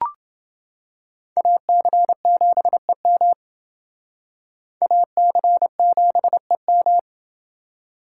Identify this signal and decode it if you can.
Morse code, heard as a keyed tone, about 29 wpm, 700 Hz. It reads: AC7EM AC7EM